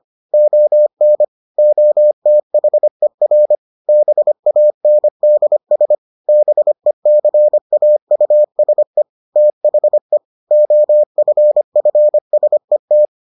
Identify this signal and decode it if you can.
Morse code, heard as a keyed tone, about 25 wpm, 600 Hz. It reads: ON OTHER BANDS BECAUSE THE OFFSET